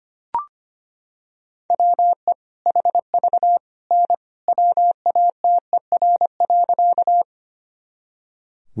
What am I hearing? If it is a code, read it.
Morse code, heard as a keyed tone, about 25 wpm, 700 Hz. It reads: WE HV N WATER.
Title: Sentences from Top 100 Words and Abbreviations 25wpm